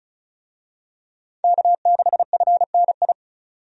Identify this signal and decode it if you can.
Morse code, heard as a keyed tone, about 34 wpm, 700 Hz. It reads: K6FNI
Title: USA Call Signs - 5 Characters - 35wpm